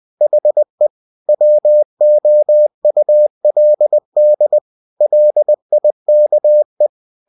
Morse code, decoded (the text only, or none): HE WOULD LIKE